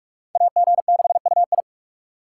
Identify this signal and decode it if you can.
Morse code, heard as a keyed tone, about 58 wpm, 700 Hz. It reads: AG6UI